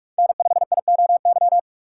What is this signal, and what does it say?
Morse code, heard as a keyed tone, about 43 wpm, 700 Hz. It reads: N5IOY
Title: USA Call Signs - 5 Characters - 4x Character Spacing - 45wpm